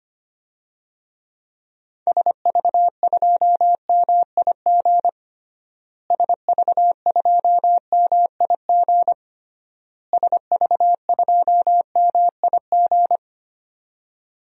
Morse code, heard as a keyed tone, about 25 wpm, 700 Hz. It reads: SV2MIG SV2MIG SV2MIG
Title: DXCC-100 Call Signs - 6 Characters - Repeated 3x - 25wpm